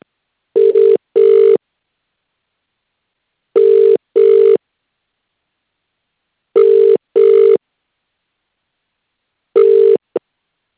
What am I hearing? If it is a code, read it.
Ringback tone (UK), 400 + 450 Hz.